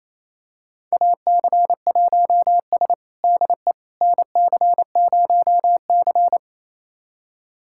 Morse code, heard as a keyed tone, about 28 wpm, 700 Hz. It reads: AC1S DE NC0C
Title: USA Callsign DE Callsign - 4 Characters - 28wpm